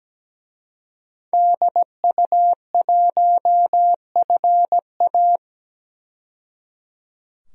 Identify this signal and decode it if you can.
Morse code, heard as a keyed tone, about 17 wpm, 700 Hz. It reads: DU1FA
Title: DXCC-100 Call Signs - 5 Characters - 17wpm